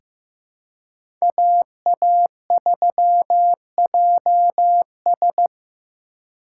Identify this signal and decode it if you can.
Morse code, heard as a keyed tone, about 16 wpm, 700 Hz. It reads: AA3JS
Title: USA Call Signs - 5 Characters - 3x Character Spacing - 15wpm